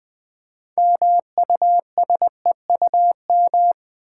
Morse code, heard as a keyed tone, about 20 wpm, 700 Hz. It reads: MUSEUM
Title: Build A Word - 3 Letters At A Time - 6-Letter Words - 20wpm